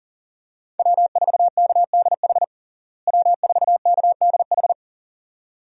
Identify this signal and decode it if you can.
Morse code, heard as a keyed tone, about 40 wpm, 700 Hz. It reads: W4KDH W4KDH
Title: USA Call Signs - 5 Characters - Repeated 2x - 40wpm